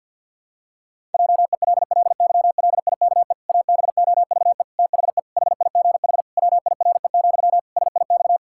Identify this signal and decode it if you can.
Morse code, heard as a keyed tone, about 52 wpm, 700 Hz. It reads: JELLYLIKE ABOVE THE HIGH WIRE, SIX
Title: Pangram Challenge v2 - 50wpm